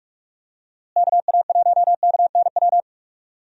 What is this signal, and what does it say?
Morse code, heard as a keyed tone, about 43 wpm, 700 Hz. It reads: KA1KNW
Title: USA Call Signs - 6 Characters - Repeated 2x - 45wpm